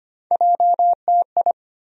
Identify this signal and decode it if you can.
Morse code, heard as a keyed tone, about 25 wpm, 700 Hz. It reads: JTI
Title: USA Call Sign Postfixes - 3 Letters 25wpm